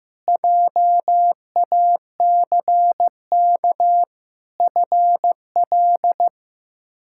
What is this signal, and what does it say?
Morse code, heard as a keyed tone, about 15 wpm, 700 Hz. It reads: JACK FL
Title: K1USN SST QSO Practice - 15wpm